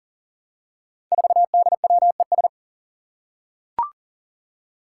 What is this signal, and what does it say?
Morse code, heard as a keyed tone, about 38 wpm, 700 Hz. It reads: 4DWES
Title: Five Challenging Letter-Number Combinations 40wpm